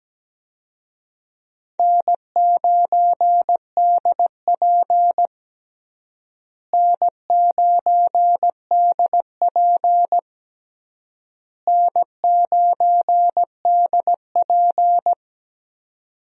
Morse code, heard as a keyed tone, about 17 wpm, 700 Hz. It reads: N9DP N9DP N9DP
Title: USA Call Signs - 4 Characters - Repeated 3x - 17wpm